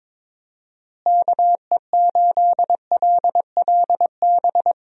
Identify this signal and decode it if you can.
Morse code, heard as a keyed tone, about 21 wpm, 700 Hz. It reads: KE8LLB
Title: USA Call Signs - 6 Characters - Repeated 2x - 22wpm